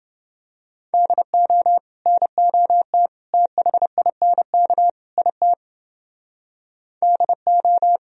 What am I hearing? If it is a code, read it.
Morse code, heard as a keyed tone, about 29 wpm, 700 Hz. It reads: DO NOT THINK IT DO